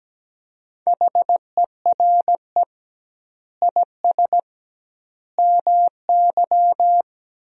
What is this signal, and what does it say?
Morse code, heard as a keyed tone, about 17 wpm, 700 Hz. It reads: HERE IS MY
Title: 2x Word-Spacing - Sentences from Top 200 Words - 17wpm